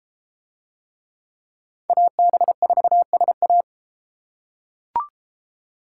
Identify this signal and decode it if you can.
Morse code, heard as a keyed tone, about 33 wpm, 700 Hz. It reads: AB4SA